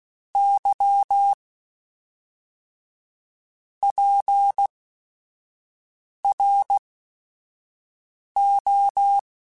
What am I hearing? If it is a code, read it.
Morse code, heard as a keyed tone, about 16 wpm, spaced as for 5 wpm, 800 Hz. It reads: Y PRO
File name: Technical Subjects / Technical Subjects-mp3s / Sw Pwr Sup @5